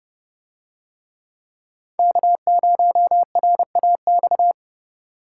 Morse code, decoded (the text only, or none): K0RAX